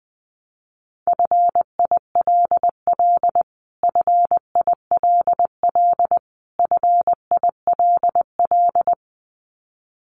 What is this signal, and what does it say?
Morse code, heard as a keyed tone, about 20 wpm, 700 Hz. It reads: FILL FILL FILL
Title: Top 500 Words Repeated 20wpm